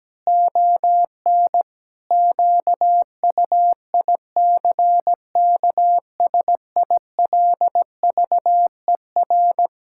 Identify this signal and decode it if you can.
Morse code, heard as a keyed tone, about 17 wpm, 700 Hz. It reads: ON QUICKSILVER
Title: Pangram Challenge v2 - 17wpm